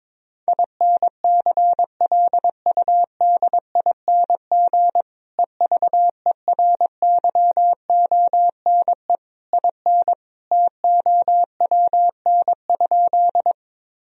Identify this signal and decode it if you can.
Morse code, heard as a keyed tone, about 22 wpm, 700 Hz. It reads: INCLUDING EVERYONE IN TOWN?
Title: Sentences from Top 1000 Words - Review Part 4 - 22wpm